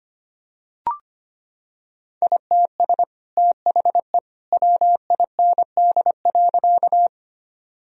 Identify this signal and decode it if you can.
Morse code, heard as a keyed tone, about 25 wpm, 700 Hz. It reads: ITS THE WIND.
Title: Sentences from Top 1000 Words - Review Part 2 - 25wpm